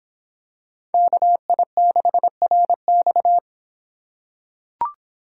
Morse code, heard as a keyed tone, about 25 wpm, 700 Hz. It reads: KI6RX